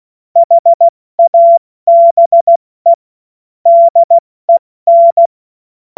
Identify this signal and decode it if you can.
Morse code, heard as a keyed tone, about 16 wpm, 680 Hz. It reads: HABE DEN
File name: Morsekurs / Witze / Witz 06. April 2021, 70 F80